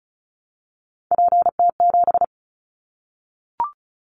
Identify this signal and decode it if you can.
Morse code, heard as a keyed tone, about 36 wpm, 700 Hz. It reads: PT7
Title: Callsign Prefixes for Top 50 DXCC - 35wpm